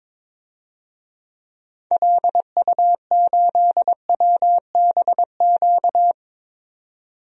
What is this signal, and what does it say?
Morse code, heard as a keyed tone, about 22 wpm, 700 Hz. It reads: LU8WBQ
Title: DXCC-100 Call Signs - 6 Characters - 22wpm